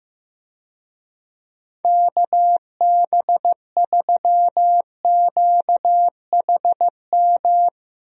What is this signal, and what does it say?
Morse code, heard as a keyed tone, about 15 wpm, 700 Hz. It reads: KB3QHM
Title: USA Call Signs - 6 Characters - Repeated 3x - 15wpm